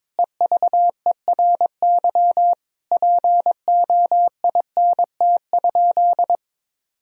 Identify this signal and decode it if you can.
Morse code, heard as a keyed tone, about 22 wpm, 700 Hz. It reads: EVERY POINT?
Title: Sentences from Top 1000 Words - Review Part 1 - 22wpm